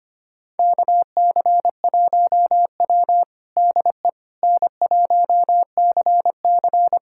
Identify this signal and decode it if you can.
Morse code, heard as a keyed tone, about 25 wpm, 700 Hz. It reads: KC1W DE N1CC
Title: USA Callsign DE Callsign - 4 Characters - 25wpm